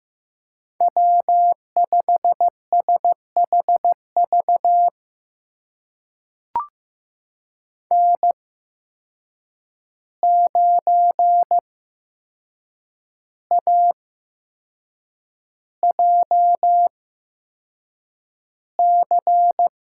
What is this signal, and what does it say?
Morse code, heard as a keyed tone, about 15 wpm, spaced as for 6 wpm, 700 Hz. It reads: W5SHV N9AJC
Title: USA Call Signs - 5 Characters - 8x Character Spacing - 15wpm